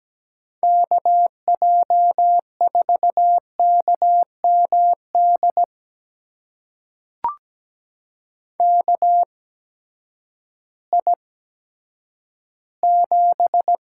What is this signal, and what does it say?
Morse code, heard as a keyed tone, about 17 wpm, spaced as for 10 wpm, 700 Hz. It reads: KJ4KMD KI7
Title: USA Call Signs - 6 Characters - 8x Character Spacing - 17wpm